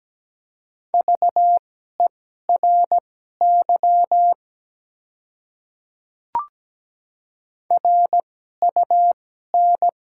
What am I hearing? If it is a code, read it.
Morse code, heard as a keyed tone, about 17 wpm, spaced as for 12 wpm, 700 Hz. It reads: VERY RUN
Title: 2x Character Spacing - Top 300 Words - 17wpm